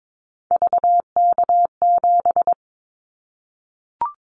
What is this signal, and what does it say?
Morse code, heard as a keyed tone, about 22 wpm, 700 Hz. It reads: VK7
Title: Callsign Prefixes for Top 100 DXCC - 22wpm